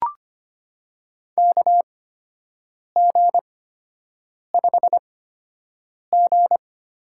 Morse code, heard as a keyed tone, about 25 wpm, spaced as for 7 wpm, 700 Hz. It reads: KG5G
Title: USA Call Signs - 8x Character Spacing - 25wpm